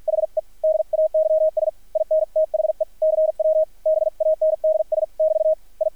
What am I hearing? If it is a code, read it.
Morse code, heard as a keyed tone, about 53 wpm, spaced as for 46 wpm, 630 Hz, with radio noise. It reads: HE DAYS IN THE CW BANDS <BT> I
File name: archived / CRSnet